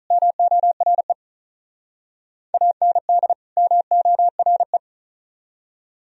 Morse code, heard as a keyed tone, about 45 wpm, 700 Hz. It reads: MORE AND MORE